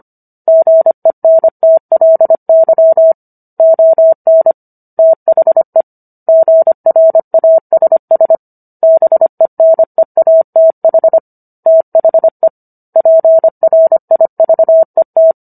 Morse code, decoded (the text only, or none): GENTLY ON THE GRASS BENEATH THE PRIVET